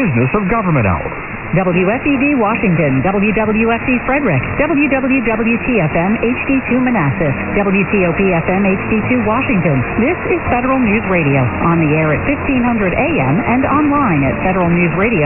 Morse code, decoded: none

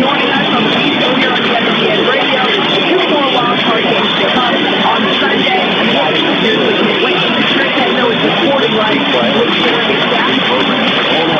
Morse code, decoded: none

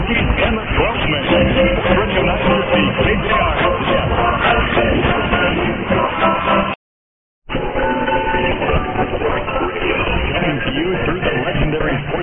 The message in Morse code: none

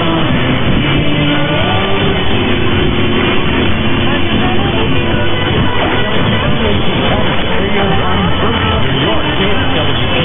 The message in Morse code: none

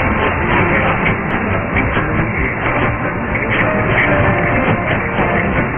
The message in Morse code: none

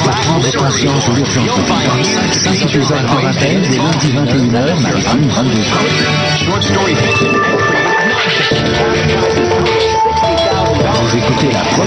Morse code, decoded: none

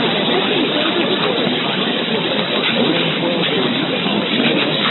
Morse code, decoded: none

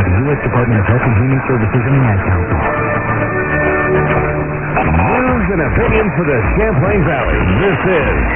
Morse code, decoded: none

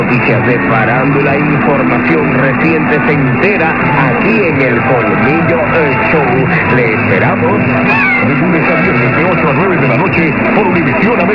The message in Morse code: none

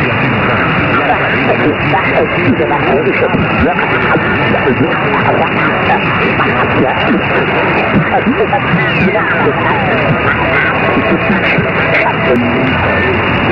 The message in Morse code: none

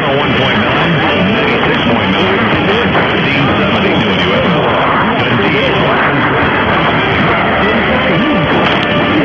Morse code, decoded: none

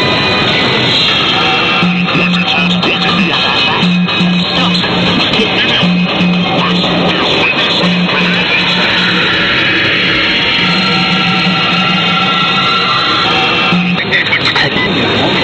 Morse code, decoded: none

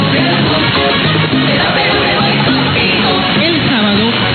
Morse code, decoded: none

